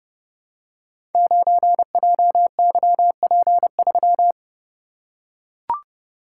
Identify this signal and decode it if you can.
Morse code, heard as a keyed tone, about 30 wpm, 700 Hz. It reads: 9JYP3
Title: Code Groups - 5 Letter:Numbers - 30wpm